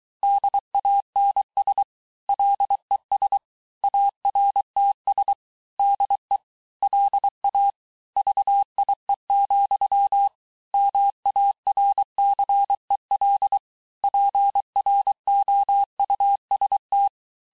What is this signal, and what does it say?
Morse code, heard as a keyed tone, about 23 wpm, 800 Hz. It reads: DANS LES ARTS DE LA VIE, MARCEL PROUST
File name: cours-du-lundi files / cours-1594--teste-clair-vitesse-17mpm-puis-22mpm